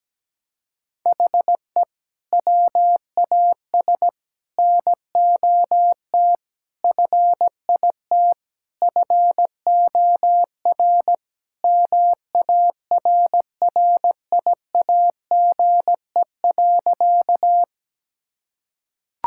Morse code, heard as a keyed tone, about 17 wpm, 700 Hz. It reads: HE WAS NOT FIT FOR MARRIAGE.
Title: Sentences from Top 1000 Words - Review Part 5 - 17wpm